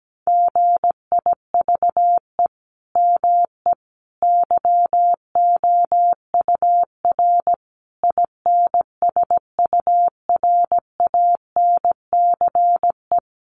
Morse code, heard as a keyed tone, about 17 wpm, 700 Hz. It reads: GIVE ME YOUR INSURANCE
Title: Sentences from Top 1100 Words 17wpm